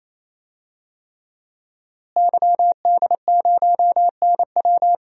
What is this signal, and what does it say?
Morse code, heard as a keyed tone, about 28 wpm, 690 Hz. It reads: YD0NW
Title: DXCC-100 Call Signs - 5 Characters - Repeated 3x - 28wpm